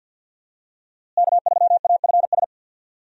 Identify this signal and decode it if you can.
Morse code, heard as a keyed tone, about 51 wpm, 700 Hz. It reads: K3AFS
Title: USA Call Signs - 5 Characters - Repeated 2x - 50wpm